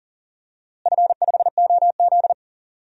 Morse code, heard as a keyed tone, about 40 wpm, 700 Hz. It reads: F5OZ
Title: DXCC-100 Call Signs - 4 Characters - Repeated 2x - 40wpm